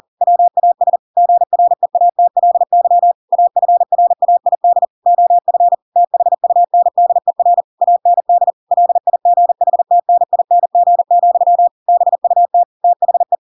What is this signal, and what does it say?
Morse code, heard as a keyed tone, about 40 wpm, 700 Hz. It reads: WAS GREATLY AFRAID OF THUNDER AND LIGHTNING, BUT THE